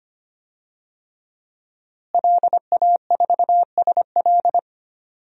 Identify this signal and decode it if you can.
Morse code, heard as a keyed tone, about 25 wpm, 700 Hz. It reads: LA4SL
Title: DXCC-100 Call Signs - 5 Characters - Repeated 3x - 25wpm